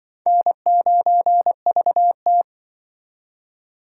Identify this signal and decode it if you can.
Morse code, heard as a keyed tone, about 25 wpm, 700 Hz. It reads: N9VT
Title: Speed Racing - USA 1x2 Call Signs 20wpm